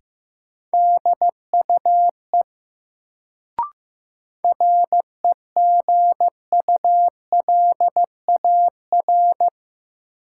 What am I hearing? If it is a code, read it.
Morse code, heard as a keyed tone, about 15 wpm, 700 Hz. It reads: DUE REGULAR
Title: Sentences from Top 900 Words - New Word Drill - 15wpm